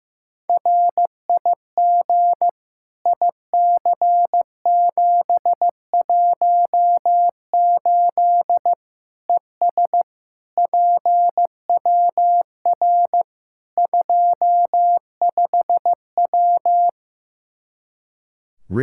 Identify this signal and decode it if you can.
Morse code, heard as a keyed tone, about 15 wpm, 700 Hz. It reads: RIG IC718 ES PWR 25W
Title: Practice QSO - Exchange 2 - Info - 15wpm